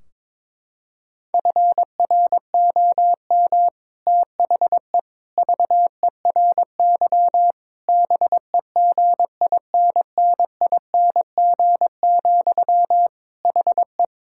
Morse code, heard as a keyed tone, about 21 wpm, 700 Hz. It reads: FROM THE VERY BEGINNING, HE
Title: Sentences from Top 1600 Words 22wpm